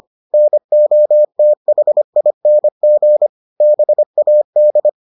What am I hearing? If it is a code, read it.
Morse code, heard as a keyed tone, about 25 wpm, 600 Hz. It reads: NOTHING BAD